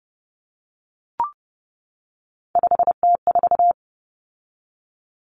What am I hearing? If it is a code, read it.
Morse code, heard as a keyed tone, about 30 wpm, 700 Hz. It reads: 5T4
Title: Common Cut Numbers - 3-Digit 30wpm